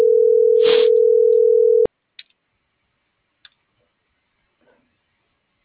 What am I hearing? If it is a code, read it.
Ringback tone (North America), 440 + 480 Hz.